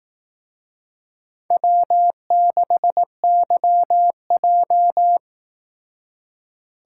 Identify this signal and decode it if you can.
Morse code, heard as a keyed tone, about 18 wpm, 700 Hz. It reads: W6YJ
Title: Speed Racing - USA 1x2 Call Signs 15wpm